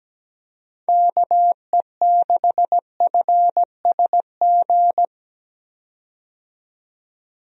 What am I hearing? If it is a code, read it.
Morse code, heard as a keyed tone, about 17 wpm, 700 Hz. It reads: KE6FSG
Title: USA Call Signs - 6 Characters - 17wpm